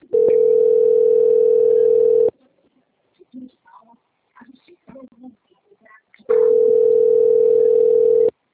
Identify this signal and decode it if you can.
Ringback tone (North America), 440 + 480 Hz, with speech.
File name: out- - - - - .47102